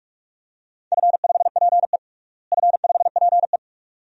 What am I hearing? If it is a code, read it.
Morse code, heard as a keyed tone, about 46 wpm, 700 Hz. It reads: F5PE F5PE
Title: DXCC-100 Call Signs - 4 Characters - Repeated 2x - 45wpm